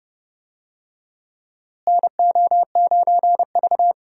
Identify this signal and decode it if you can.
Morse code, heard as a keyed tone, about 29 wpm, 700 Hz. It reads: NO9V
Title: USA Call Signs - 4 Characters - Repeated 2x - 30wpm